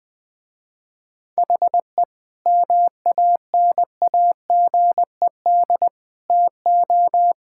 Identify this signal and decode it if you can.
Morse code, heard as a keyed tone, about 21 wpm, 700 Hz. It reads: HE MANAGED TO
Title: Sentences with 5-Word Phrases - Spoken Beginning - 20wpm